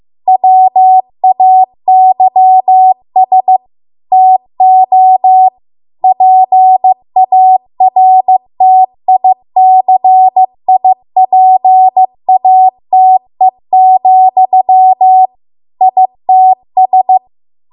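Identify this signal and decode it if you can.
Morse code, heard as a keyed tone, about 15 wpm, 750 Hz. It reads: WAYS TO PARTICIPATE, ITS
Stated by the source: Morse code practice at 15 wpm from the national amateur radio association's code archive.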